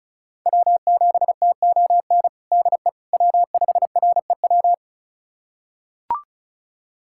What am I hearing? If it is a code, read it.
Morse code, heard as a keyed tone, about 34 wpm, 700 Hz. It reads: W7TON DE W5REW
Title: USA Callsign DE Callsign - 5 Characters - 35wpm